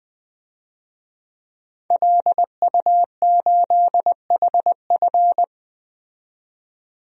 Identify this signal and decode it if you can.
Morse code, heard as a keyed tone, about 21 wpm, 700 Hz. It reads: LU8HF
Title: DXCC-100 Call Signs - 5 Characters - 20wpm